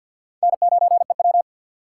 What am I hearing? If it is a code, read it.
Morse code, heard as a keyed tone, about 51 wpm, 700 Hz. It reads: N9EW
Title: USA Call Signs - 4 Characters - 3x Character Spacing - 50wpm